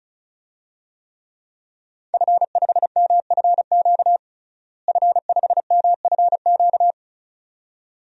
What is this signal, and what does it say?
Morse code, heard as a keyed tone, about 34 wpm, 700 Hz. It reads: F5MFQ F5MFQ
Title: DXCC-100 Call Signs - 5 Characters - Repeated 2x - 35wpm